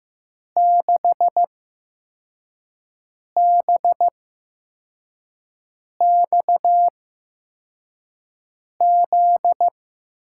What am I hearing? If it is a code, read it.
Morse code, heard as a keyed tone, about 15 wpm, spaced as for 4 wpm, 700 Hz. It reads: 6BXZ
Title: USA Call Signs - 6 Characters - 8x Character Spacing - 15wpm